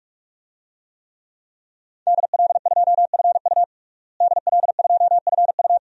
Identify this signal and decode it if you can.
Morse code, heard as a keyed tone, about 45 wpm, 700 Hz. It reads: DL2FU DL2FU
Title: DXCC-100 Call Signs - 5 Characters - Repeated 2x - 45wpm